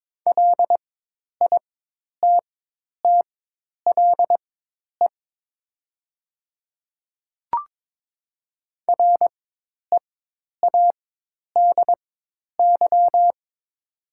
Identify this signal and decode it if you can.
Morse code, heard as a keyed tone, about 22 wpm, spaced as for 10 wpm, 700 Hz. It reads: LITTLE READY
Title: 4x Character Spacing - Top 500 Words - 22wpm